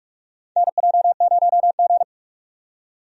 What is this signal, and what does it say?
Morse code, heard as a keyed tone, about 46 wpm, 700 Hz. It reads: NJ0G
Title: USA Call Signs - 4 Characters - 45wpm